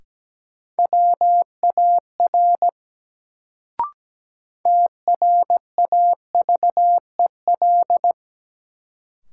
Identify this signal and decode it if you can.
Morse code, heard as a keyed tone, about 17 wpm, 700 Hz. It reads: WAR TRAVEL
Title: Sentences from Top 400 Words - New Word Drill - 17wpm